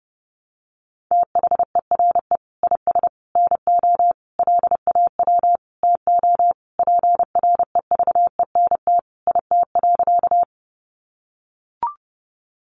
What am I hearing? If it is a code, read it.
Morse code, heard as a keyed tone, about 30 wpm, 700 Hz. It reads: THERE IS NO LAW TO PREVENT IT.
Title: Sentences from Top 1100 Words 30wpm